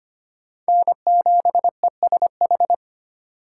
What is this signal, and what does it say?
Morse code, heard as a keyed tone, about 32 wpm, 700 Hz. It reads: N7ESH